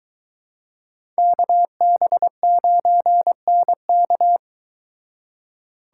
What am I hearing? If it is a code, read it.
Morse code, heard as a keyed tone, about 24 wpm, 700 Hz. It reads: KB9NK